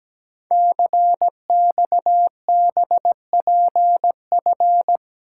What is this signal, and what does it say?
Morse code, heard as a keyed tone, about 17 wpm, 700 Hz. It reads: CXBPF
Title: Code Groups - 5 Letters - 17wpm